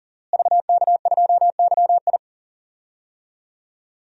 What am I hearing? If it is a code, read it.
Morse code, heard as a keyed tone, about 40 wpm, 700 Hz. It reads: VK2YI
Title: DXCC-100 Call Signs - 5 Characters - Repeated 2x - 40wpm